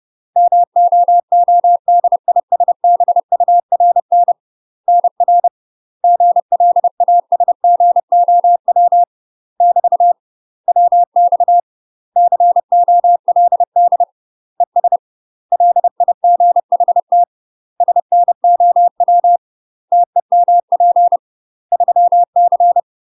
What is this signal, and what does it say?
Morse code, heard as a keyed tone, about 30 wpm, 700 Hz. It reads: MOODISBURN NR GLASGOW <BT> WX COLD ES LIGHT SNOW TEMP 3C
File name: UK MASTER QSOs TRAINING/30WPM